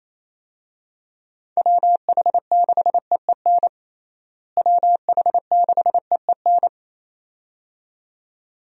Morse code, heard as a keyed tone, about 28 wpm, 700 Hz. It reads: WH6EEN WH6EEN
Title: USA Call Signs - 6 Characters - Repeated 2x - 28wpm